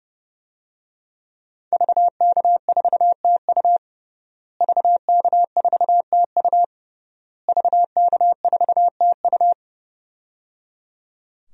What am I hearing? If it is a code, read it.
Morse code, heard as a keyed tone, about 30 wpm, 700 Hz. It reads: VK4TU VK4TU VK4TU
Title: DXCC-100 Call Signs - 5 Characters - Repeated 3x - 30wpm